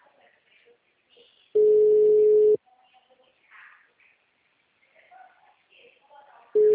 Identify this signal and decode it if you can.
Ringback tone (Europe, 425 Hz), with speech.